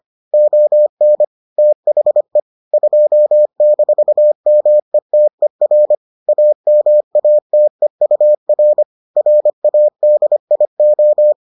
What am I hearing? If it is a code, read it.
Morse code, heard as a keyed tone, about 25 wpm, 600 Hz. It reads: ON THE 2-METER AMATEUR RADIO